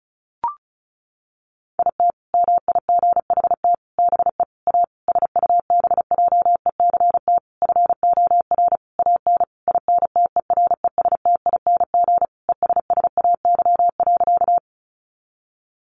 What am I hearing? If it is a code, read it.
Morse code, heard as a keyed tone, about 34 wpm, 700 Hz. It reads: IT MIGHT BE A SUBJECT FOR AN INTERESTING ESSAY.
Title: Sentences from Top 1500 Words 35wpm